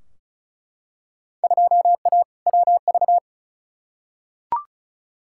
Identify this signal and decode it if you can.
Morse code, heard as a keyed tone, about 36 wpm, 690 Hz. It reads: 2A WV
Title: Field Day Exchanges Encore - 35wpm